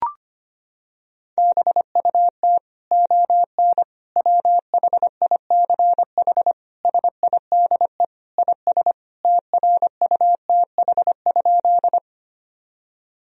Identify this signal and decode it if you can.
Morse code, heard as a keyed tone, about 25 wpm, 700 Hz. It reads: BUT ON WHICH SIDE IS TRUTH?
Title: Sentences from Top 1000 Words - Review Part 1 - 25wpm